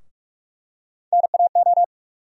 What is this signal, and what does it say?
Morse code, heard as a keyed tone, about 41 wpm, 700 Hz. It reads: NAO
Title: USA Call Sign Postfixes - 3 Letters 45wpm